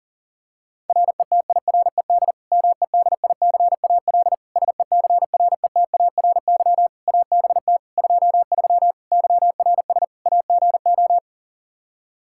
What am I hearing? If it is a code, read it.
Morse code, heard as a keyed tone, about 40 wpm, 700 Hz. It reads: RETIRED MEDICAL SECRETARY ABT 23 YRS AGO
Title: K1USN SST QSO Practice 40wpm